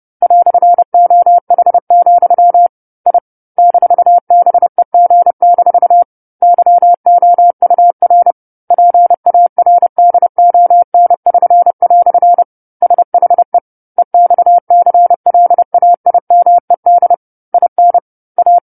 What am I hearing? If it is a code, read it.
Morse code, heard as a keyed tone, about 30 wpm, 700 Hz. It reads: "OH, I -BEG- YOUR PARDON<SN>" SHE EXCLAIMED IN A
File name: 10-Chapter-10 30 WPM